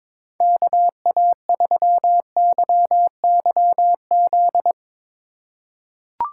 Morse code, heard as a keyed tone, about 22 wpm, 700 Hz. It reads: KA3YYZ
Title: USA Call Signs - 4x Character Spacing - 22wpm